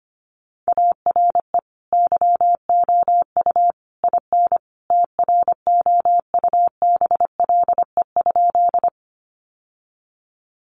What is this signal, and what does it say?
Morse code, heard as a keyed tone, about 25 wpm, 700 Hz. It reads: ARE YOU IN TROUBLE?
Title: Sentences from Top 1200 Words 25wpm